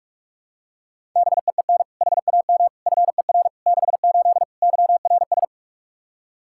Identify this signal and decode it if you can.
Morse code, heard as a keyed tone, about 46 wpm, 700 Hz. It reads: BEEN HAM FER 68 YRS
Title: Practice QSO - Exchange 3 - More Info - 45wpm